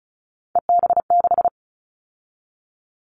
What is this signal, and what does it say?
Morse code, heard as a keyed tone, about 36 wpm, 690 Hz. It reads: EB6
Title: Callsign Prefixes for Top 50 DXCC - 35wpm